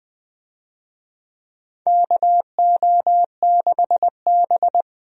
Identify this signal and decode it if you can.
Morse code, heard as a keyed tone, about 20 wpm, 700 Hz. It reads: KO6B
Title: USA Call Signs - 4 Characters - 20wpm